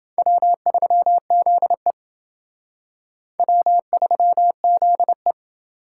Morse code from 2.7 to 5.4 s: W3ZE

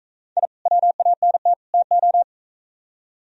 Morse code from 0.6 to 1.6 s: WANT

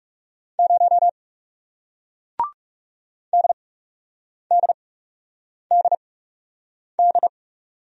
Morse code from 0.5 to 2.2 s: 0

